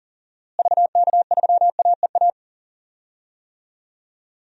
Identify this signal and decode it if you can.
Morse code, heard as a keyed tone, about 40 wpm, 700 Hz. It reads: VK3AEA
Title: DXCC-100 Call Signs - 6 Characters - Repeated 2x - 40wpm